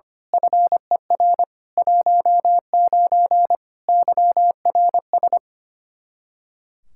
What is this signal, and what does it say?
Morse code, heard as a keyed tone, about 25 wpm, 700 Hz. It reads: FER 19 YRS
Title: Practice QSO - Exchange 3 - More Info - 25wpm